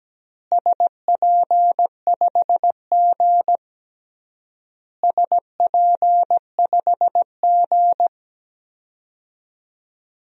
Morse code, heard as a keyed tone, about 17 wpm, 700 Hz. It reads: SP5G SP5G
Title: DXCC-100 Call Signs - 4 Characters - Repeated 3x - 17wpm